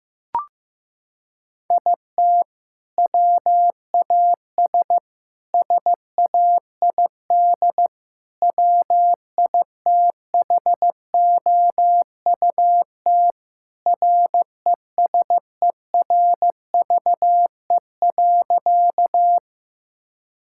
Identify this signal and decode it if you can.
Morse code, heard as a keyed tone, about 15 wpm, 700 Hz. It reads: IT WAS SAID WITHOUT RESERVE.
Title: Sentences from Top 1000 Words - Review Part 6 - 15wpm